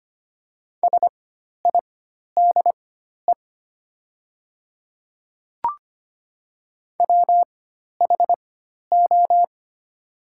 Morse code, heard as a keyed tone, about 25 wpm, spaced as for 12 wpm, 700 Hz. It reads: SIDE WHO
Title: 4x Character Spacing - Top 100 Words - 25wpm